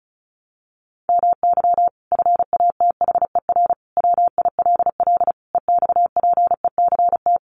Morse code, heard as a keyed tone, about 34 wpm, 700 Hz. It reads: MY FATHER WILL EXPECT
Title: Sentences from Top 1100 Words 35wpm